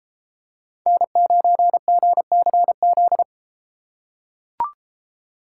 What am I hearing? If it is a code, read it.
Morse code, heard as a keyed tone, about 33 wpm, 700 Hz. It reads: N9GCZ